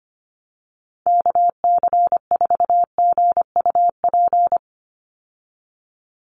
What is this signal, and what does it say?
Morse code, heard as a keyed tone, about 25 wpm, 700 Hz. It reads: KC4GUP
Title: USA Call Signs 25wpm